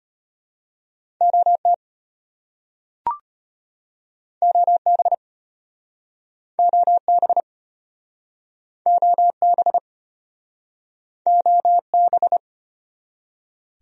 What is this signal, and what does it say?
Morse code, heard as a keyed tone, about 34 wpm, 700 Hz. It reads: OT OB OB OB OB